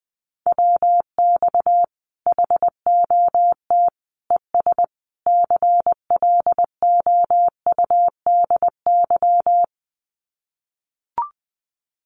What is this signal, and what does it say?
Morse code, heard as a keyed tone, about 20 wpm, 700 Hz. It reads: WX HOT ES CLOUDY
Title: Common QSO Phrases 20wpm